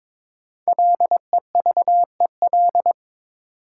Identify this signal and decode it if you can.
Morse code, heard as a keyed tone, about 21 wpm, 700 Hz. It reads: LEVEL
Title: Sentences from Top 700 Words - New Word Drill - 22wpm